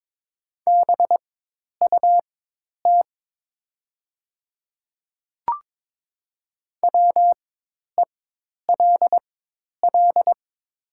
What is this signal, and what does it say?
Morse code, heard as a keyed tone, about 22 wpm, spaced as for 10 wpm, 700 Hz. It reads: BUT WELL
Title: 4x Character Spacing - Top 300 Words - 22wpm